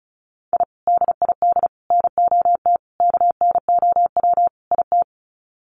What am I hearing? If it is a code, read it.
Morse code, heard as a keyed tone, about 34 wpm, 700 Hz. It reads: I DID NOT KNOW IT
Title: Sets of 5 Words Encore2 35wpm